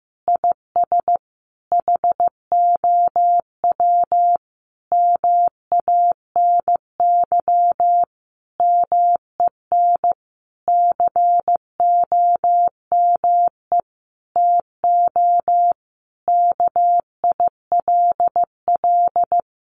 Morse code, heard as a keyed tone, about 14 wpm, 700 Hz. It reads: IS HOW MANY MEN COME TO KILL
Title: Sentences from Top 1400 Words 15wpm